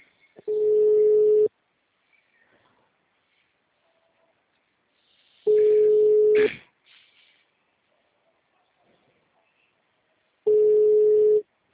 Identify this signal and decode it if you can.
Ringback tone (Europe, 425 Hz).